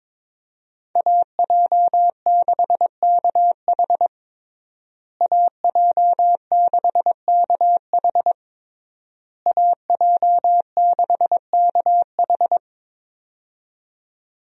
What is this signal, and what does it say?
Morse code, heard as a keyed tone, about 22 wpm, 700 Hz. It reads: AJ6KH AJ6KH AJ6KH
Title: USA Call Signs - 5 Characters - Repeated 3x - 22wpm